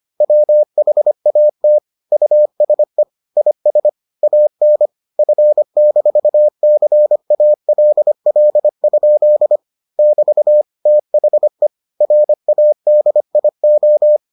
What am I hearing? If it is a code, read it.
Morse code, heard as a keyed tone, about 25 wpm, 600 Hz. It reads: WHAT USE IS AN F-CALL? <BT> THE RADIO